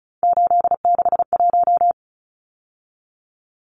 Morse code, heard as a keyed tone, about 34 wpm, 700 Hz. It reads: 861
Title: Numbers - 3-Digit 35wpm